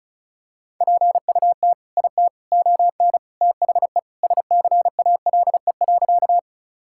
Morse code, heard as a keyed tone, about 34 wpm, 700 Hz. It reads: PUT IT ON THE SCALE.
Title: Sentences from Top 1000 Words - Review Part 5 - 35wpm